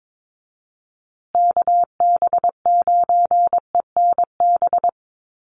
Morse code, heard as a keyed tone, about 22 wpm, 700 Hz. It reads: KB9ENB
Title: USA Call Signs Repeated 3x 22wpm